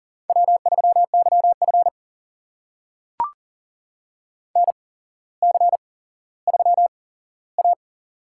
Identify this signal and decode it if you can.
Morse code, heard as a keyed tone, about 38 wpm, spaced as for 17 wpm, 700 Hz. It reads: W3YF NC3A
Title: USA Call Signs - 4 Characters - 8x Character Spacing - 40wpm